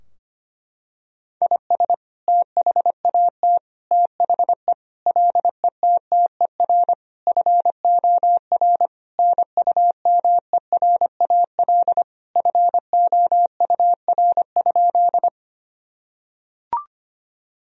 Morse code, heard as a keyed tone, about 25 wpm, 700 Hz. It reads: IS THAT THE LETTER FOR NUMERAL FOUR?
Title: Sentences from Top 1000 Words - Review Part 2 - 25wpm